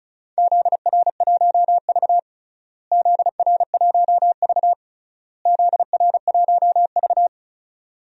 Morse code, heard as a keyed tone, about 34 wpm, 700 Hz. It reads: ZR1V ZR1V ZR1V
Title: DXCC-100 Call Signs - 4 Characters - Repeated 3x - 35wpm